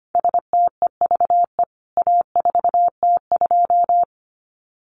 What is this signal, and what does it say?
Morse code, heard as a keyed tone, about 25 wpm, 700 Hz. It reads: STEVE A4T2
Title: CWT Exchange 25wpm